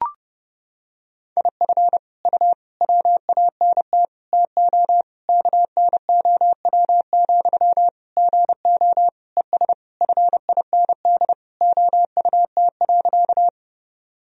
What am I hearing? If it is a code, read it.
Morse code, heard as a keyed tone, about 30 wpm, 700 Hz. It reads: IF U WANT TO KNOW, GO ES FIND OUT.
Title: Sentences from Top 200 Words and Abbreviations 30wpm